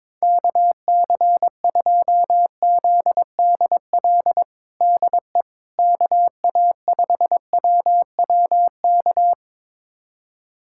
Morse code, heard as a keyed tone, about 21 wpm, 700 Hz. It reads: KC2ZDL DE KA5WWK
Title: USA Callsign DE Callsign 22wpm